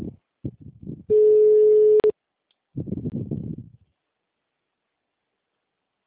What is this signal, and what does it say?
Ringback tone (Europe, 425 Hz).
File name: out- -2011- - - .48473